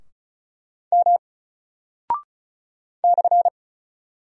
Morse code, heard as a keyed tone, about 36 wpm, 700 Hz. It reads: M /
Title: Single Character (Letter:Number:Punctuation:Prosign) Alt 35wpm